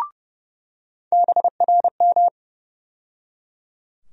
Morse code, heard as a keyed tone, about 29 wpm, 700 Hz. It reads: BRM
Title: USA Call Sign Postfixes - 3 Letters 30wpm